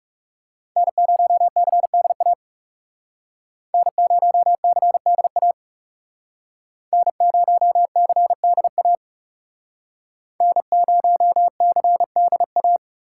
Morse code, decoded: N0CDA N0CDA N0CDA N0CDA

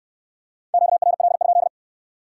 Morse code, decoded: QRL?

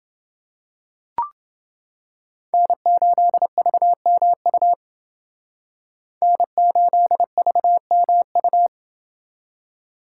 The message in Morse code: N8VMU N8VMU

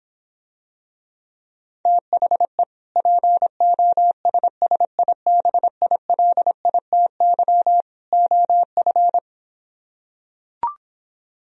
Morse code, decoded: THE POSSIBILITY OF

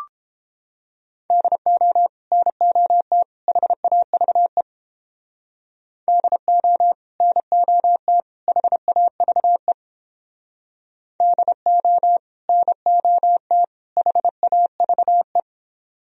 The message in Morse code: DO NOT HAVE DO NOT HAVE DO NOT HAVE

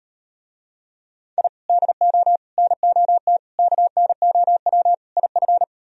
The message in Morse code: I DO NOT KNOW IF

